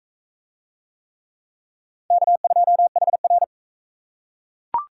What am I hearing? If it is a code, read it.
Morse code, heard as a keyed tone, about 44 wpm, 700 Hz. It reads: K2HR